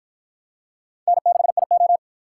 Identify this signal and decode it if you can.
Morse code, heard as a keyed tone, about 51 wpm, 700 Hz. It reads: N6IO